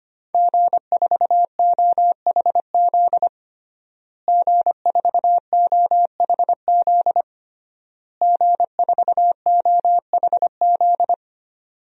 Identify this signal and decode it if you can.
Morse code, heard as a keyed tone, about 25 wpm, 700 Hz. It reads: G4OHZ G4OHZ G4OHZ
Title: DXCC-100 Call Signs - 5 Characters - Repeated 3x - 25wpm